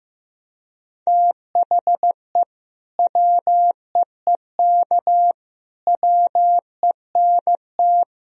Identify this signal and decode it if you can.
Morse code, heard as a keyed tone, about 15 wpm, 700 Hz. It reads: THE WEEK WENT
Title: Sentences from Top 1000 Words - Review Part 2 - 15wpm